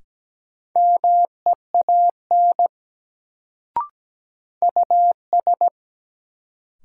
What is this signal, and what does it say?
Morse code, heard as a keyed tone, about 17 wpm, 700 Hz. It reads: MEAN US
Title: Sentences from Top 200 Words - New Word Drill - 17wpm